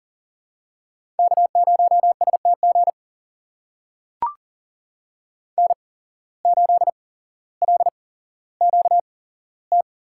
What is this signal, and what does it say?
Morse code, heard as a keyed tone, about 39 wpm, spaced as for 17 wpm, 700 Hz. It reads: K0STG N8LQT
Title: USA Call Signs - 8x Character Spacing - 40wpm